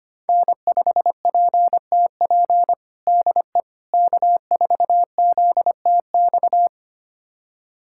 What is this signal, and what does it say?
Morse code, heard as a keyed tone, about 25 wpm, 700 Hz. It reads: N5PTP DE K4ZTX
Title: USA Callsign DE Callsign - 5 Characters - 25wpm